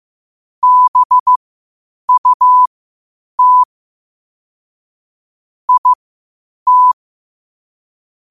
Morse code, heard as a keyed tone, about 15 wpm, spaced as for 8 wpm, 1000 Hz. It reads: BUT IT
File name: qotdfiles / 8wpm